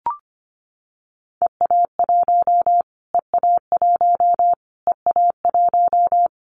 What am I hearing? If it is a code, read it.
Morse code, heard as a keyed tone, about 25 wpm, 700 Hz. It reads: EA1 EA1 EA1
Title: Callsign Prefixes for Top 25 DXCC - Repeated 3x - 25wpm